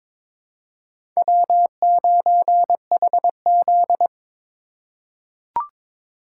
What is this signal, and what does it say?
Morse code, heard as a keyed tone, about 21 wpm, 700 Hz. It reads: W9HZ
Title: USA Call Signs - 4 Characters - 2x Character Spacing - 22wpm